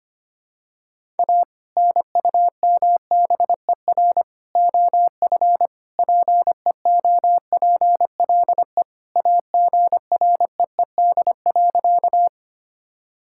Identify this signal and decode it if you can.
Morse code, heard as a keyed tone, about 25 wpm, 700 Hz. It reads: A NUMBER OF PEOPLE AGREED.
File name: practice / 25wpm / Sentences with 3-Word Phrases - 25wpm